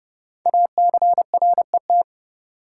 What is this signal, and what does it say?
Morse code, heard as a keyed tone, about 30 wpm, 690 Hz. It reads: ACRET
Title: Code Groups - 5 Letters - 30wpm